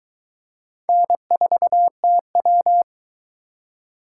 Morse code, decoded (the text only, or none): N4TW